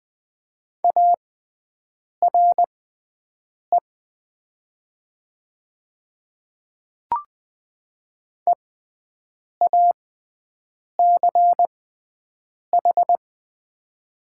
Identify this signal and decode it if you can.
Morse code, heard as a keyed tone, about 20 wpm, spaced as for 7 wpm, 700 Hz. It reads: ARE EACH
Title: 6x Character Spacing - Top 100 Words - 20wpm